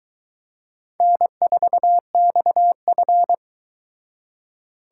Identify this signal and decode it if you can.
Morse code, heard as a keyed tone, about 23 wpm, 700 Hz. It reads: N4XF